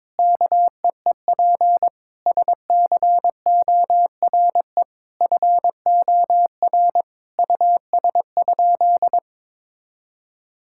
Morse code, heard as a keyed tone, about 22 wpm, 700 Hz. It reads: KEEP SCORE FOR US?
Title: Sentences from Top 1600 Words 22wpm